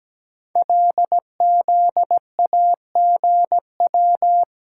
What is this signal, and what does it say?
Morse code, heard as a keyed tone, about 17 wpm, 700 Hz. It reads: LZAGW
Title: Code Groups - 5 Letter:Numbers - 17wpm